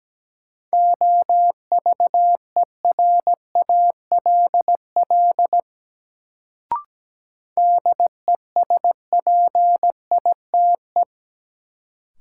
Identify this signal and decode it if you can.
Morse code, heard as a keyed tone, about 17 wpm, 700 Hz. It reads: OVERALL DESPITE
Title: Sentences from Top 1000 Words - Review Part 7 - New Word Drill - 17wpm